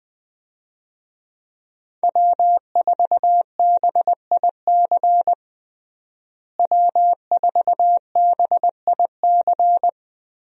Morse code, heard as a keyed tone, about 21 wpm, 700 Hz. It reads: W4BIC W4BIC
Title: USA Call Signs - 5 Characters - Repeated 3x - 20wpm